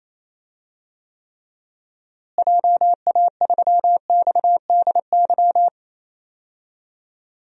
Morse code, heard as a keyed tone, about 28 wpm, 700 Hz. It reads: JA3XDY
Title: DXCC-100 Call Signs - 6 Characters - Repeated 2x - 28wpm